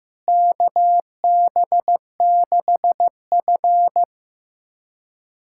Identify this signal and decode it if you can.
Morse code, heard as a keyed tone, about 15 wpm, 700 Hz. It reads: KB6F
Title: USA Callsign DE Callsign 15wpm